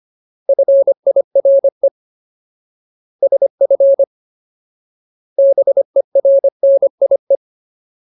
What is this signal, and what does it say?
Morse code, heard as a keyed tone, about 25 wpm, 550 Hz. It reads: FIRE SF BERNIE
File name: Ultimate Word List / 25WPM Ultimate